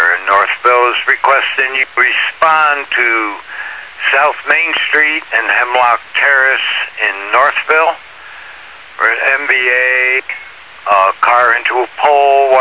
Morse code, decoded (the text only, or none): none